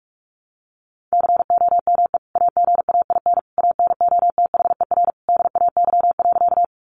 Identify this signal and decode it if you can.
Morse code, heard as a keyed tone, about 44 wpm, 700 Hz. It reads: COME AGAIN ANOTHER DAY.
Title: Sentences from Top 600 Words 45wpm